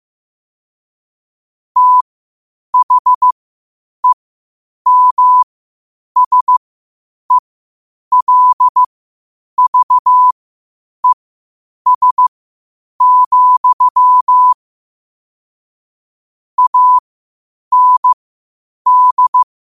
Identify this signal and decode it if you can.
Morse code, heard as a keyed tone, about 15 wpm, spaced as for 8 wpm, 1000 Hz. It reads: THEMSELVES, AND
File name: qotdfiles / 8wpm